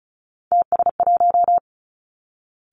Morse code, heard as a keyed tone, about 36 wpm, 700 Hz. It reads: TS1
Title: Three Challenging Letter-Number Combinations 35wpm